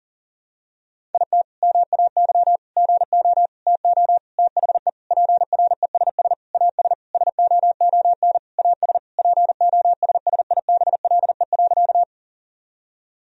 Morse code, heard as a keyed tone, about 40 wpm, 700 Hz. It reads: IT MAY GO TO THE PRESS AS SOON AS POSSIBLE.
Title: Sentences from Top 1000 Words - Review Part 1 - 40wpm